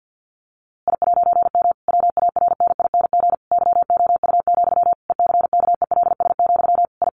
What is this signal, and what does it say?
Morse code, heard as a keyed tone, about 50 wpm, 700 Hz. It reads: I'M WARNING YOU, ECKELS, I
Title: A Sound Of Thunder 50wpm